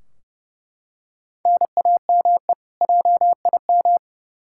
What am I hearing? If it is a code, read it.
Morse code, heard as a keyed tone, about 30 wpm, 700 Hz. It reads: NAME JIM
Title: K1USN SST QSO Practice 30wpm